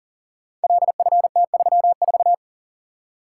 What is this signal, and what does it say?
Morse code, heard as a keyed tone, about 38 wpm, 700 Hz. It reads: LFT34
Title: Code Groups - 5 Letter:Numbers - 40wpm